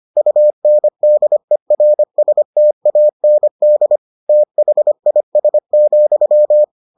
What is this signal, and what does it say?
Morse code, heard as a keyed tone, about 25 wpm, 600 Hz. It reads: UNDERSTAND THIS,